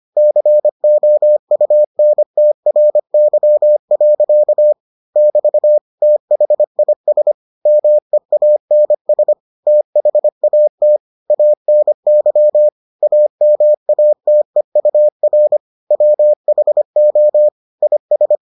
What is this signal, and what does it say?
Morse code, heard as a keyed tone, about 25 wpm, 600 Hz. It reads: COUNTRY. <BT> THIS MEANS THAT ANY AMATEUR WHO IS